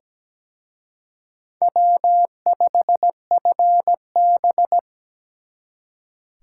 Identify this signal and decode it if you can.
Morse code, heard as a keyed tone, about 20 wpm, 700 Hz. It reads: W5FB
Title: Speed Racing - USA 1x2 Call Signs 17wpm